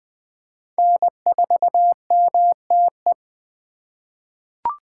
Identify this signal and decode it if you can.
Morse code, heard as a keyed tone, about 20 wpm, 700 Hz. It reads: N4MTE
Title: 2023 POTA Activator Call Signs - 20wpm